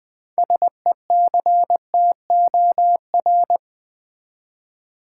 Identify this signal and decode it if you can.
Morse code, heard as a keyed tone, about 21 wpm, 700 Hz. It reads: SECTOR
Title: Six Letter Words - 20wpm